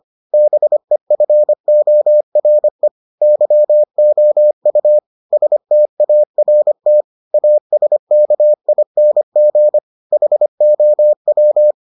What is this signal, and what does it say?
Morse code, heard as a keyed tone, about 25 wpm, 600 Hz. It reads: BEFORE YOU START ASKING HOW